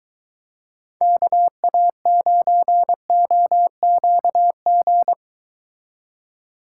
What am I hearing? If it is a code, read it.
Morse code, heard as a keyed tone, about 23 wpm, 700 Hz. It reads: KA9OQG